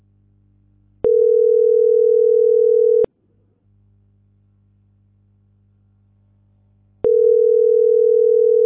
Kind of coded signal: telephone call progress tone